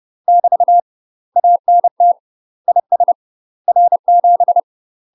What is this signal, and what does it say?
Morse code, heard as a keyed tone, about 30 wpm, 700 Hz. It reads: <BT> ANT IS R7
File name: UK MASTER QSOs TRAINING / 30WPM / UK MASTER QSOs 0024